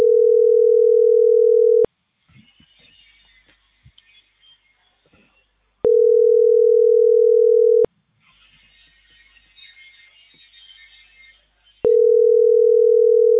Ringback tone (North America), 440 + 480 Hz.